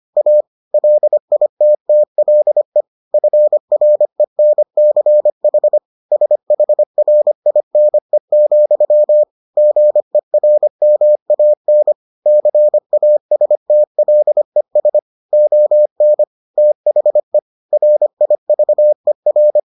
Morse code, heard as a keyed tone, about 25 wpm, 600 Hz. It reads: A LITTLE FRENCH SHRINE, GERMAN CASTLES ON THE RIVER